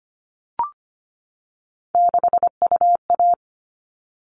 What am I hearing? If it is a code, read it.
Morse code, heard as a keyed tone, about 25 wpm, 700 Hz. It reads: 6UA
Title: All Cut Numbers - 3-Digit 25wpm